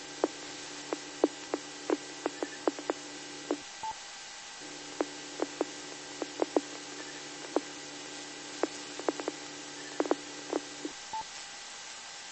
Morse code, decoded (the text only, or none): none